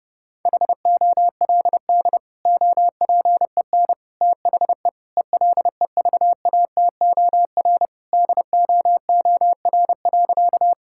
Morse code, heard as a keyed tone, about 30 wpm, 700 Hz. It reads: HOLD OPEN THE ELEVATOR DOOR.
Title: Sentences from Top 1600 Words 30wpm